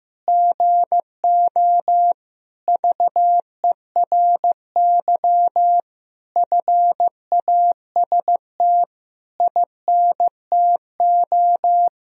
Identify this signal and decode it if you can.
Morse code, heard as a keyed tone, about 16 wpm, 700 Hz. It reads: GO VERY FAST INTO
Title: Sentences from Top 1000 Words - Review Part 2 - 15wpm